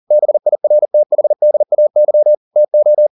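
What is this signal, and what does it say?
Morse code, heard as a keyed tone, about 40 wpm, 600 Hz. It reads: BIRTHDAY TO
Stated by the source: Morse code at 40 wpm, practice text, CW med Gnister.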